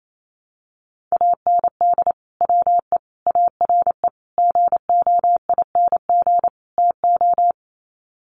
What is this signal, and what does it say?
Morse code, heard as a keyed tone, about 28 wpm, 700 Hz. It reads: AND WE ARE GOING TO
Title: Sets of 5 Words Encore2 28wpm